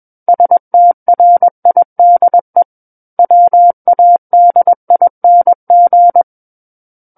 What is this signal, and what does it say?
Morse code, heard as a keyed tone, about 21 wpm, 700 Hz. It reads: STRIDE WADING
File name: Book One / 12-Chapter 12 - 21 WPM